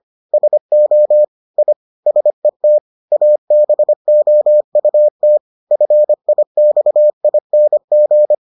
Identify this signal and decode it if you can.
Morse code, heard as a keyed tone, about 25 wpm, 600 Hz. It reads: SO I SET ABOUT FIXING